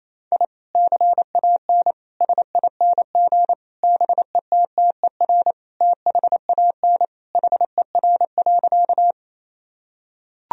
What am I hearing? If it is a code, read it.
Morse code, heard as a keyed tone, about 28 wpm, 700 Hz. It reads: I CAN SING BETTER THAN HER.
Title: Sentences from Top 1000 Words - Review Part 2 - 28wpm